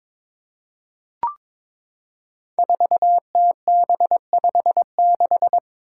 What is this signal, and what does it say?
Morse code, heard as a keyed tone, about 22 wpm, 700 Hz. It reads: 4TB56
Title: Five Challenging Letter-Number Combinations 22wpm